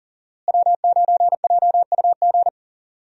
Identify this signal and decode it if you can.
Morse code, heard as a keyed tone, about 37 wpm, 700 Hz. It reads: W9JUG
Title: USA Call Signs - 5 Characters - Repeated 3x - 40wpm